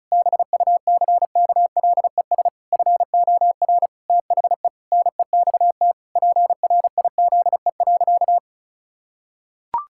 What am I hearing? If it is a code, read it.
Morse code, heard as a keyed tone, about 34 wpm, 700 Hz. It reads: BUCKLES FOR THE NEXT PRIZE.
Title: Pangram Challenge v2 - 35wpm